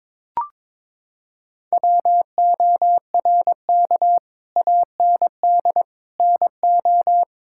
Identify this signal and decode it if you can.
Morse code, heard as a keyed tone, about 22 wpm, 700 Hz. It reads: WORK AND NO
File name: Sentences with 7-Word Phrases - 22wpm